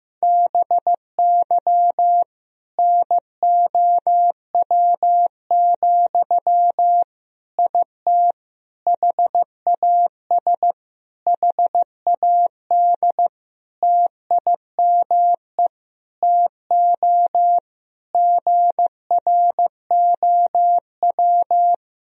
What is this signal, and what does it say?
Morse code, heard as a keyed tone, about 15 wpm, 700 Hz. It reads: BY NOW, IT HAS HAD TIME TO GROW
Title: Sentences from Top 1000 Words - Review Part 1 - 15wpm